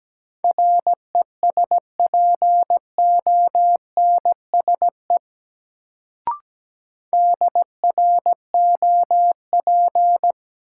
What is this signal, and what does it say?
Morse code, heard as a keyed tone, about 17 wpm, 700 Hz. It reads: RESPONSE DROP
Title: Sentences from Top 1400 Words - New Word Drill - 17wpm